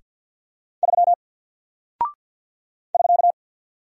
Morse code, heard as a keyed tone, about 48 wpm, 700 Hz. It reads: 3 <SK>
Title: Single Character (Letter:Number:Punctuation:Prosign) 50wpm